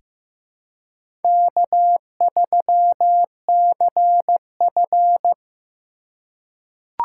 Morse code, heard as a keyed tone, about 15 wpm, 700 Hz. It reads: K3CF
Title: USA Call Signs - 4 Characters - 3x Character Spacing - 15wpm